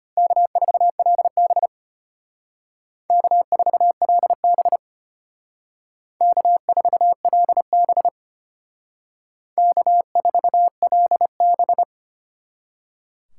Morse code, heard as a keyed tone, about 32 wpm, 700 Hz. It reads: K4LB K4LB K4LB K4LB